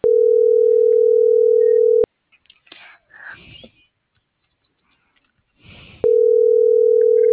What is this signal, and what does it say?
Ringback tone (North America), 440 + 480 Hz.